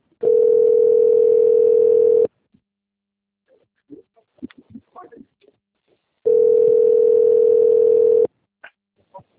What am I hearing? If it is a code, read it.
Ringback tone (North America), 440 + 480 Hz.